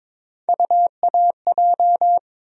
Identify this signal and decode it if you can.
Morse code, heard as a keyed tone, about 22 wpm, 700 Hz. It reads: UAJ
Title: USA Call Sign Postfixes - 3 Letters 22wpm